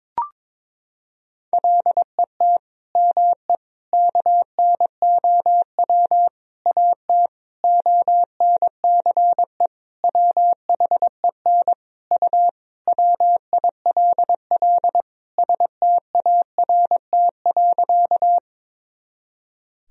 Morse code, heard as a keyed tone, about 22 wpm, 700 Hz. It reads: LET ME KNOW AT ONCE WHEN U WILL START.
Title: Sentences from Top 300 Words and Abbreviations 22wpm